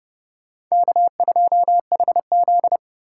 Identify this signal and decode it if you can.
Morse code, heard as a keyed tone, about 36 wpm, 700 Hz. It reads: K2HZ
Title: Speed Racing - USA 1x2 Call Signs 30wpm